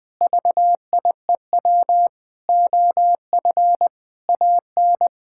Morse code, heard as a keyed tone, about 20 wpm, 700 Hz. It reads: VIEW OF AN
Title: Sentences with 6-Word Phrases - Spoken Beginning - 20wpm